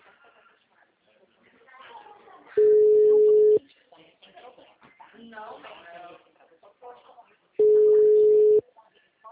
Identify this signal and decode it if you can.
Ringback tone (Europe, 425 Hz), with speech.
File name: out- -2009- - - .8561